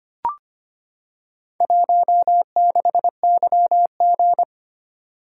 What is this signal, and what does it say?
Morse code, heard as a keyed tone, about 25 wpm, 700 Hz. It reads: 16YG
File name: practice / 25wpm / Four Letter-Number Combinations - 25wpm